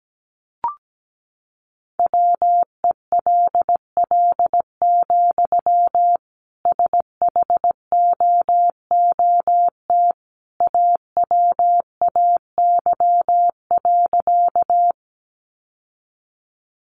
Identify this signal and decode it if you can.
Morse code, heard as a keyed tone, about 17 wpm, 700 Hz. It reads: WELL, SHOOT AWAY.
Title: Sentences from Top 1200 Words 17wpm